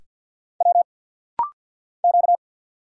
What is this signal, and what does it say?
Morse code, heard as a keyed tone, about 50 wpm, 700 Hz. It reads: W Q
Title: Single Letters - Rapid-Fire 50wpm